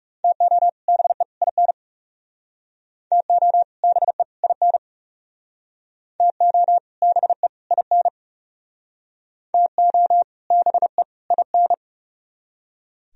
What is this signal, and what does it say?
Morse code, heard as a keyed tone, about 38 wpm, 700 Hz. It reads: TO BE IN TO BE IN TO BE IN TO BE IN